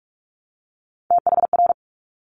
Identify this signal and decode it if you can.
Morse code, heard as a keyed tone, about 46 wpm, 690 Hz. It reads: THR
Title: Top Three Letter Combinations in Words 45wpm